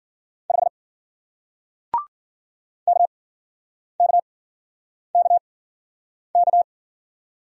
Morse code, heard as a keyed tone, about 50 wpm, spaced as for 9 wpm, 700 Hz. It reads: 5 KKKK
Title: Speed-Racing Single Letter-Number 40wpm